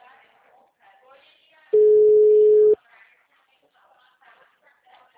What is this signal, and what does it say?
Ringback tone (Europe, 425 Hz), with speech.
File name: out- -2006- - - .96419